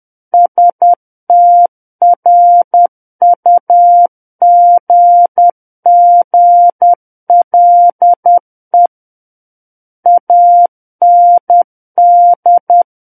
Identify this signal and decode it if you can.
Morse code, heard as a keyed tone, about 10 wpm, 700 Hz. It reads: STRUGGLE AND
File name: Book One / 01-Chapter 1 - 10 WPM